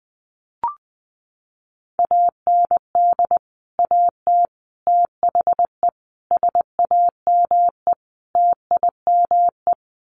Morse code, heard as a keyed tone, about 20 wpm, 700 Hz. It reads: AND AT THE SAME TIME
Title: Sets of 5 Words 20wpm